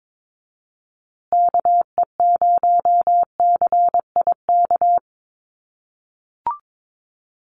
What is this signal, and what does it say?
Morse code, heard as a keyed tone, about 22 wpm, 700 Hz. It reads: KE0CIK
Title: USA Call Signs Repeated 3x 22wpm